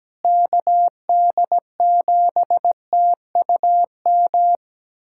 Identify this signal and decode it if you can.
Morse code, heard as a keyed tone, about 17 wpm, 700 Hz. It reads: KD7TUM
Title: USA Callsign DE Callsign - 6 Characters - 17wpm